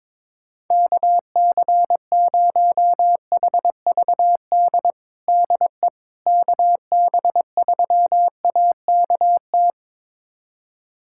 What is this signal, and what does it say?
Morse code, heard as a keyed tone, about 21 wpm, 700 Hz. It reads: KC0HVD DE KB3AKT
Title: USA Callsign DE Callsign 22wpm